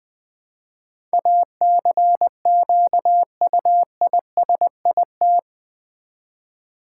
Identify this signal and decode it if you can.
Morse code, heard as a keyed tone, about 20 wpm, 700 Hz. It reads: ACQUISIT
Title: Build A Word - 4 Letters At A Time - 11-Letter Words - 20wpm